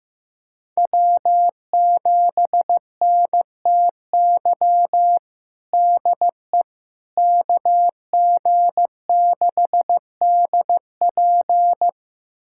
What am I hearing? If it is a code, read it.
Morse code, heard as a keyed tone, about 15 wpm, 700 Hz. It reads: W7NTY DE KG6DP
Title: USA Callsign DE Callsign - 5 Characters - 15wpm